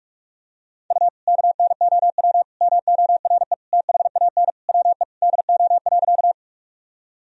Morse code, heard as a keyed tone, about 42 wpm, 700 Hz. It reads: U KNOW MORE THAN WE DO.
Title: Sentences from Top 100 Words and Abbreviations 45wpm